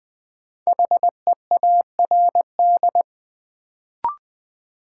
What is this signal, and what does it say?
Morse code, heard as a keyed tone, about 21 wpm, 700 Hz. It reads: HEARD
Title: Sentences from Top 400 Words - New Word Drill - 20wpm